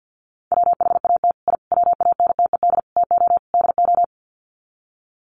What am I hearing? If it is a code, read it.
Morse code, heard as a keyed tone, about 50 wpm, 700 Hz. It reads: WHAT I WANTED TO DO
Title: Sets of 5 Words Encore2 50wpm